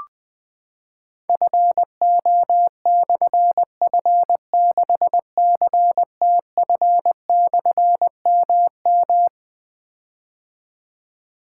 Morse code, heard as a keyed tone, about 20 wpm, 700 Hz. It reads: FO/F6CTF/MM
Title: Difficult Call Signs 20wpm